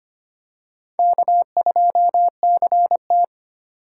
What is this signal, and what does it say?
Morse code, heard as a keyed tone, about 25 wpm, 700 Hz. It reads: K2CT
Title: USA Call Signs - 4x Character Spacing - 25wpm